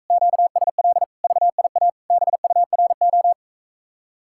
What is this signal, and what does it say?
Morse code, heard as a keyed tone, about 45 wpm, 700 Hz. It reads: QSL VIA BURO